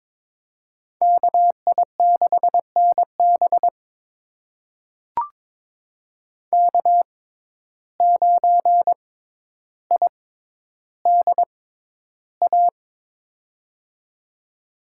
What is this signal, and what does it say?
Morse code, heard as a keyed tone, about 22 wpm, spaced as for 11 wpm, 700 Hz. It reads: KI6NB K9IDA
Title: USA Call Signs - 5 Characters - 6x Character Spacing - 22wpm